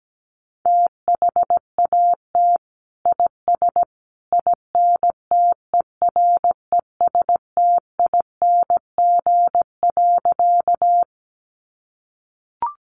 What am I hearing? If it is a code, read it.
Morse code, heard as a keyed tone, about 17 wpm, 700 Hz. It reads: THAT IS INTERESTING.
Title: Sentences from Top 1000 Words 17wpm